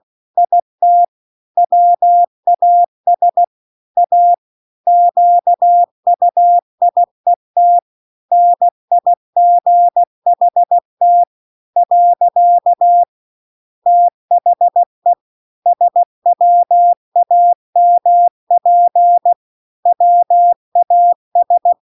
Morse code, heard as a keyed tone, about 16 wpm, 700 Hz. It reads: IT WAS A QUIET NIGHT . THE SWAMP WAS